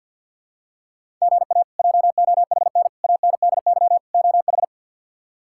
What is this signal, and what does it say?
Morse code, heard as a keyed tone, about 47 wpm, 700 Hz. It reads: GA JOHN ANDY OH
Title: K1USN SST QSO Practice - 50wpm